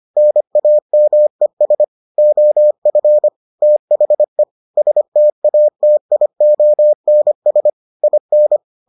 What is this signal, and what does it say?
Morse code, heard as a keyed tone, about 25 wpm, 600 Hz. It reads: NAMES OF THE STATIONS IN